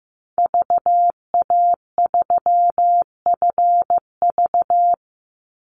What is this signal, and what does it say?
Morse code, heard as a keyed tone, about 15 wpm, 700 Hz. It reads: VA3FV
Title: Call Signs Repeated 2x - 15wpm